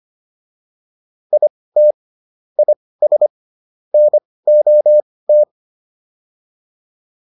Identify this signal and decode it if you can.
Morse code, heard as a keyed tone, about 25 wpm, spaced as for 18 wpm, 600 Hz. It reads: IT IS NOT